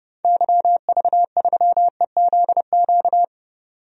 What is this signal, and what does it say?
Morse code, heard as a keyed tone, about 30 wpm, 700 Hz. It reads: YV3EZQ
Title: DXCC-100 Call Signs - 6 Characters - 30wpm